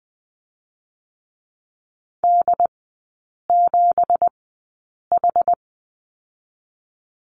Morse code, heard as a keyed tone, about 21 wpm, 700 Hz. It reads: D 7 H
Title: 172 Lesson 28e - Number 9 - Three Characters 20wpm